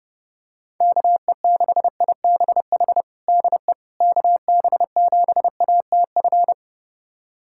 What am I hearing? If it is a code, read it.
Morse code, heard as a keyed tone, about 30 wpm, 700 Hz. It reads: KE6IBH DE KB7ATF
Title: USA Callsign DE Callsign - 6 Characters - 30wpm